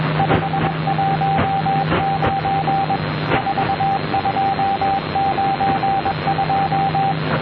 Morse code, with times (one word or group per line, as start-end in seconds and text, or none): 0.2-3.0 519
3.4-7.1 4292